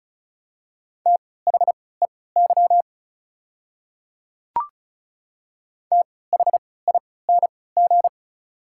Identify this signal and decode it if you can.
Morse code, heard as a keyed tone, about 36 wpm, spaced as for 20 wpm, 700 Hz. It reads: THEY THING
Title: 3x Character Spacing - Top 100 Words - 35wpm